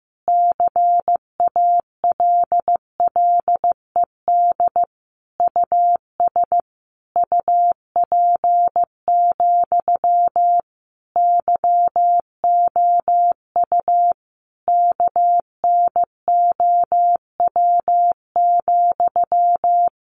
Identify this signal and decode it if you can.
Morse code, heard as a keyed tone, about 15 wpm, 700 Hz. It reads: CALLED US UP, YOU KNOW,
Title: A Sound Of Thunder 15wpm